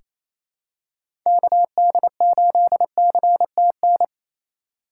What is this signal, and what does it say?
Morse code, heard as a keyed tone, about 28 wpm, 700 Hz. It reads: KD8CTN
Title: USA Call Signs - 2x Character Spacing - 28wpm